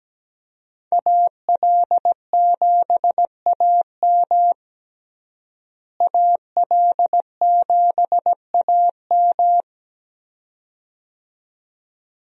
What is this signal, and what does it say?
Morse code, heard as a keyed tone, about 17 wpm, 700 Hz. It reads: AL7AM AL7AM
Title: USA Call Signs - 5 Characters - Repeated 2x - 17wpm